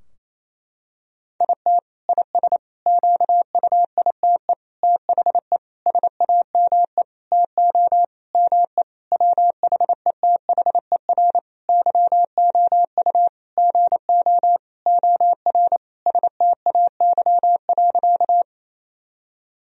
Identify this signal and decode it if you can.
Morse code, heard as a keyed tone, about 28 wpm, 700 Hz. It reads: IT IS QUITE THE SAME TO ME WHETHER YOU GO OR STAY.
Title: Sentences from Top 1000 Words - Review Part 3 - 28wpm